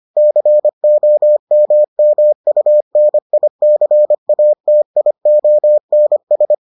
Morse code, heard as a keyed tone, about 25 wpm, 600 Hz. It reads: COMMUNICATIONS